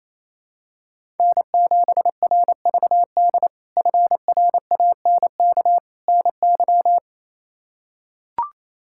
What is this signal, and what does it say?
Morse code, heard as a keyed tone, about 28 wpm, 700 Hz. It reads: N7RVD FRANK NY
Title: K1USN SST QSO Practice - 28wpm